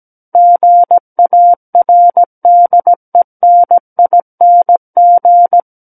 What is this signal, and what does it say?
Morse code, heard as a keyed tone, about 17 wpm, 700 Hz. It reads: GARDENING
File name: Book One / 08-Chapter 8 - 17 WPM